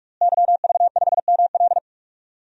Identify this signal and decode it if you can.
Morse code, heard as a keyed tone, about 47 wpm, 700 Hz. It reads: YV5ML
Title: DXCC-100 Call Signs - 5 Characters - Repeated 3x - 45wpm